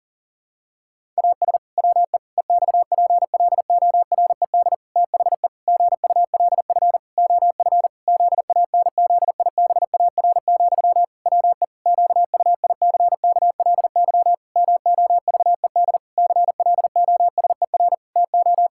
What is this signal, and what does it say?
Morse code, heard as a keyed tone, about 40 wpm, 700 Hz. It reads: AS WE EXPLORED THE GULF OF ZANZIBAR, WE QUICKLY MOVED CLOSER TO
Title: Pangram Challenge v2 - 40wpm